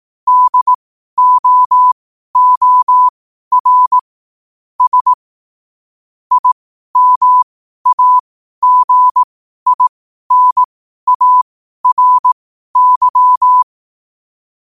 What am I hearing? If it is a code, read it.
Morse code, heard as a keyed tone, about 18 wpm, spaced as for 12 wpm, 1000 Hz. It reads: DOOR S IMAGINARY
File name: qotdfiles / 12wpm